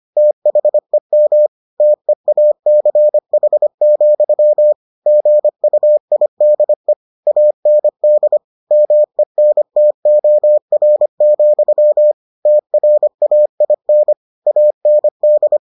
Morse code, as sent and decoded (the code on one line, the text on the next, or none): - .... -- / - . .- -.-. .... --..-- / --. ..- .. -.. / .- -. -.. / -- . -. - --- .-. --..-- / - .-. .- .. -. / .- -. -..
THEM TEACH, GUIDE AND MENTOR, TRAIN AND